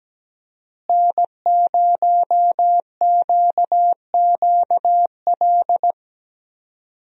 Morse code, heard as a keyed tone, about 17 wpm, 700 Hz. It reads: N0QQL
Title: USA Call Signs - 5 Characters - Repeated 3x - 17wpm